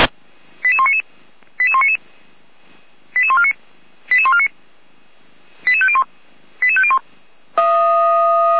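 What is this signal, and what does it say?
Five-tone selective call (ZVEI), heard as quick runs of five tones, tones of about 70 ms. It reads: 88199 88199 88149 88149 88501 88501
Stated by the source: Five-tone ZVEI selective call received on an SDR.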